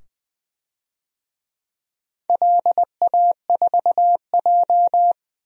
Morse code, heard as a keyed tone, about 20 wpm, 700 Hz. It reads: LA4J
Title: DXCC-100 Call Signs - 4 Characters - 20wpm